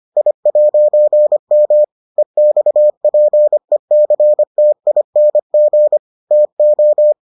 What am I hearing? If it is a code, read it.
Morse code, heard as a keyed tone, about 25 wpm, 600 Hz. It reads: I'M EXPECTING TO